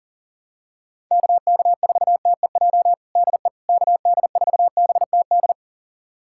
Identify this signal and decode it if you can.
Morse code, heard as a keyed tone, about 40 wpm, 700 Hz. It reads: KK4TEJ DE KD4BTD
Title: USA Callsign DE Callsign - 6 Characters - 40wpm